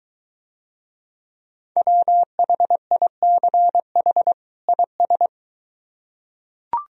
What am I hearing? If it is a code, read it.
Morse code, heard as a keyed tone, about 23 wpm, 700 Hz. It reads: WHICH IS